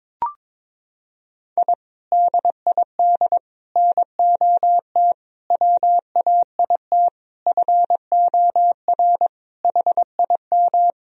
Morse code, heard as a keyed tone, about 22 wpm, 700 Hz. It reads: I DID NOT WAIT FOR HIM
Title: Sentences from Top 1000 Words - Review Part 2 - 22wpm